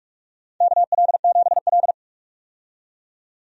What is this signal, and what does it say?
Morse code, heard as a keyed tone, about 44 wpm, 700 Hz. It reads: KL7L
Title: DXCC-100 Call Signs - 4 Characters - 45wpm